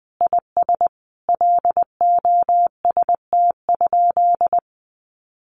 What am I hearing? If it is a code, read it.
Morse code, heard as a keyed tone, about 20 wpm, 700 Hz. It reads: IS LOST?
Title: Sentences from Top 1400 Words 20wpm